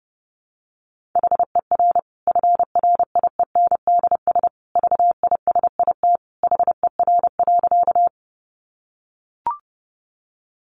Morse code, heard as a keyed tone, about 30 wpm, 700 Hz. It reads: HER FRIENDS VISIT HER.
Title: Easy Phrases 30wpm